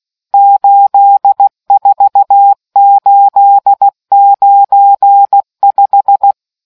Morse code, heard as a keyed tone, about 16 wpm, 800 Hz. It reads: 84895